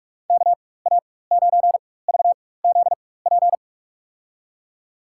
Morse code, heard as a keyed tone, about 46 wpm, spaced as for 21 wpm, 700 Hz. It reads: KA9VZP
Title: USA Call Signs - 4x Character Spacing - 45wpm